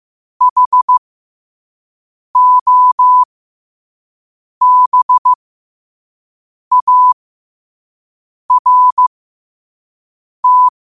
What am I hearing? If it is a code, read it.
Morse code, heard as a keyed tone, about 15 wpm, spaced as for 5 wpm, 1000 Hz. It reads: HOBART
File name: qotdfiles / 5wpm